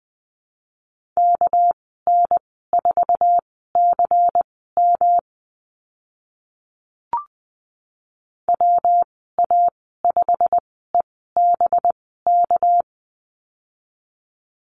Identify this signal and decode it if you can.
Morse code, heard as a keyed tone, about 20 wpm, spaced as for 15 wpm, 700 Hz. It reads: KN4CM WA5EBK
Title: 200 Lesson 32g - Letter k - Call Signs 20wpm